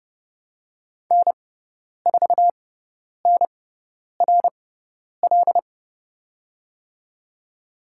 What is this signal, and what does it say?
Morse code, heard as a keyed tone, about 30 wpm, spaced as for 10 wpm, 700 Hz. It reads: N4NRL
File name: Lesson 18g - Letter m - Call Signs 30wpm